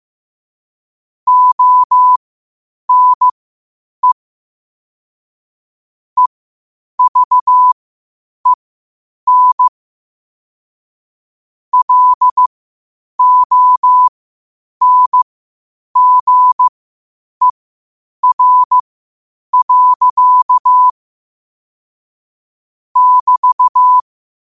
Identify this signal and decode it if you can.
Morse code, heard as a keyed tone, about 15 wpm, spaced as for 8 wpm, 1000 Hz. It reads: ONE EVEN LONGER. <BT>
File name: qotdfiles / 8wpm